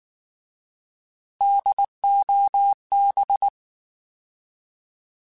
Morse code, heard as a keyed tone, about 19 wpm, 800 Hz. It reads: DOB
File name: Alphabetical / Alphabetical mp3s / Ds @18